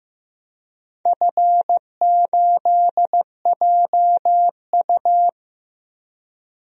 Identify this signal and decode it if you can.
Morse code, heard as a keyed tone, about 15 wpm, 700 Hz. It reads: F8JU
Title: DXCC-100 Call Signs - 4 Characters - Repeated 2x - 15wpm